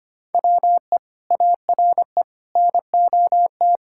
Morse code, heard as a keyed tone, about 25 wpm, 700 Hz. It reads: WE ARE NOT
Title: Sentences from Top 1000 Words - Review Part 6 - 25wpm